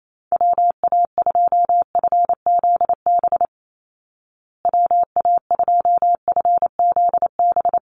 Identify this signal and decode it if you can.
Morse code, heard as a keyed tone, about 28 wpm, 700 Hz. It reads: WA2FZB WA2FZB
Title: Call Signs Repeated 2x - 28wpm